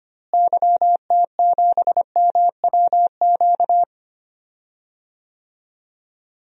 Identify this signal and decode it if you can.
Morse code, heard as a keyed tone, about 25 wpm, 700 Hz. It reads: YT7MWQ
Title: DXCC-100 Call Signs - 6 Characters - Repeated 3x - 25wpm